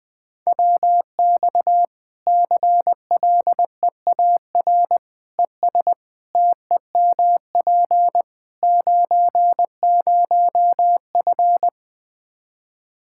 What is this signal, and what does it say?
Morse code, heard as a keyed tone, about 21 wpm, 700 Hz. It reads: WX CLEAR ES TEMP 90F
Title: Practice QSO - Exchange 2 - Info - 20wpm